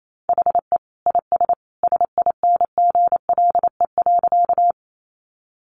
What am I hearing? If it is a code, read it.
Morse code, heard as a keyed tone, about 28 wpm, 700 Hz. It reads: HE IS SINGLE.
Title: Sentences from Top 700 Words 28wpm